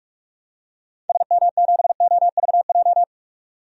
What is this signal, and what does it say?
Morse code, heard as a keyed tone, about 46 wpm, 700 Hz. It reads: SM7OVJ
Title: DXCC-100 Call Signs - 6 Characters - 45wpm